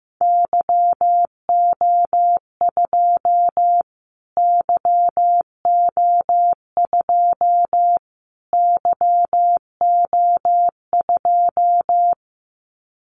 Morse code, heard as a keyed tone, about 14 wpm, 700 Hz. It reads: YO2 YO2 YO2
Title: Callsign Prefixes for Top 25 DXCC - Repeated 3x - 15wpm